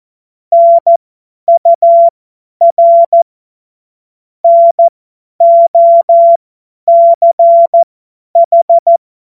Morse code, heard as a keyed tone, about 14 wpm, spaced as for 10 wpm, 680 Hz. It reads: NUR NOCH
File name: Morsekurs / Witze / Witz 02. Dezember 2025 50 F70